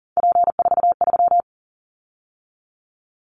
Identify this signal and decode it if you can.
Morse code, heard as a keyed tone, about 40 wpm, 700 Hz. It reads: P43
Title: Callsign Prefixes for Top 100 DXCC - 40wpm